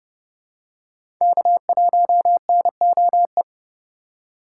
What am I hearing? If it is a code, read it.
Morse code, heard as a keyed tone, about 30 wpm, 700 Hz. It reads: K1NOE
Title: USA Call Signs - 6x Character Spacing - 30wpm